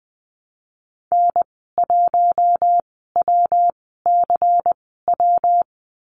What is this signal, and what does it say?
Morse code, heard as a keyed tone, about 20 wpm, spaced as for 15 wpm, 700 Hz. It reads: N1WCW
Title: 167 Lesson 27g - Number 7 - Call Signs 20wpm